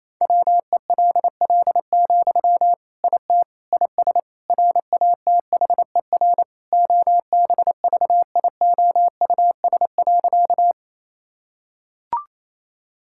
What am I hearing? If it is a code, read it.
Morse code, heard as a keyed tone, about 28 wpm, 700 Hz. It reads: WELL, IT IS RATHER OBVIOUS.
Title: Sentences from Top 1600 Words 28wpm